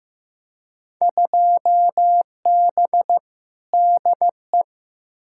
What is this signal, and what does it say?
Morse code, heard as a keyed tone, about 15 wpm, 700 Hz. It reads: 2B DE
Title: Field Day Exchanges Encore - 15wpm